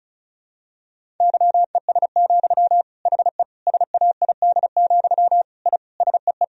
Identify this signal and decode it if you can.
Morse code, heard as a keyed tone, about 34 wpm, 700 Hz. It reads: YES, HE SAID, I SEE
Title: Sentences from Top 1000 Words - Review Part 1 - 35wpm